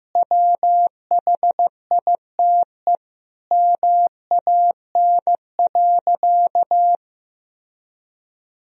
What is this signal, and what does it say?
Morse code, heard as a keyed tone, about 15 wpm, 700 Hz. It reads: WHITE MAN.
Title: Sentences from Top 1000 Words - Review Part 1 - 15wpm